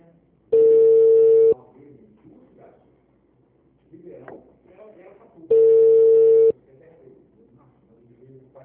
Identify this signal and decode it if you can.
Ringback tone (China, 450 Hz), with speech.
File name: out- -2006- - - .42